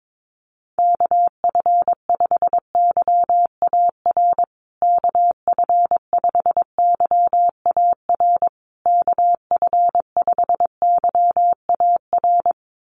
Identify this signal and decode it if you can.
Morse code, heard as a keyed tone, about 22 wpm, 700 Hz. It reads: KF5YAR KF5YAR KF5YAR
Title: USA Call Signs Repeated 3x 22wpm